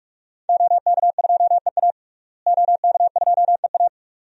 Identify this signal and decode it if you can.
Morse code, heard as a keyed tone, about 44 wpm, 700 Hz. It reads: OK2EA OK2EA
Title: DXCC-100 Call Signs - 5 Characters - Repeated 2x - 45wpm